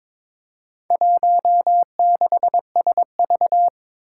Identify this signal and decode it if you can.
Morse code, heard as a keyed tone, about 22 wpm, 700 Hz. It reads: 16SV
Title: Four Challenging Letter-Number Combinations 22wpm